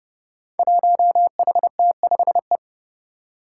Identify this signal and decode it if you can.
Morse code, heard as a keyed tone, about 29 wpm, 700 Hz. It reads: 1HT5E
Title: Five Challenging Letter-Number Combinations 30wpm